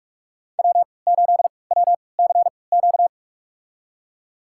Morse code, heard as a keyed tone, about 46 wpm, spaced as for 25 wpm, 700 Hz. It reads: W8WCQ
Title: USA Call Signs - 5 Characters - 3x Character Spacing - 45wpm